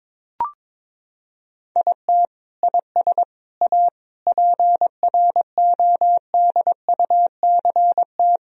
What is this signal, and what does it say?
Morse code, heard as a keyed tone, about 22 wpm, 700 Hz. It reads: IT IS A PRODUCT
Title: Sentences from Top 1000 Words - Review Part 2 - 22wpm